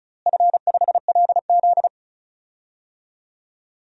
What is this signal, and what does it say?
Morse code, heard as a keyed tone, about 36 wpm, 700 Hz. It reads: F5LZ
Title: DXCC-100 Call Signs - 4 Characters - Repeated 2x - 35wpm